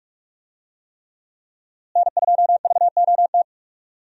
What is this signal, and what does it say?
Morse code, heard as a keyed tone, about 43 wpm, 700 Hz. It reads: N2VOT
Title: USA Call Signs - 5 Characters - Repeated 3x - 45wpm